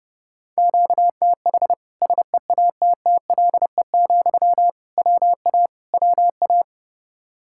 Morse code, heard as a keyed tone, about 32 wpm, 700 Hz. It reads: QTH SEATTLE, WA WA
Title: Practice QSO - Exchange 1 - Beginning - 30wpm